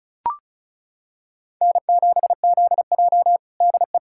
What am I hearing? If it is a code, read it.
Morse code, heard as a keyed tone, about 36 wpm, 700 Hz. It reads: N7ZJ DE
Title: USA Callsign DE Callsign - 4 Characters - 35wpm